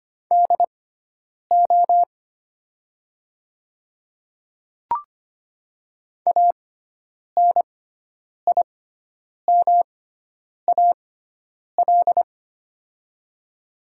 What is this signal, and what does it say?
Morse code, heard as a keyed tone, about 25 wpm, spaced as for 9 wpm, 700 Hz. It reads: DO ANIMAL
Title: 6x Character Spacing - Top 300 Words - 25wpm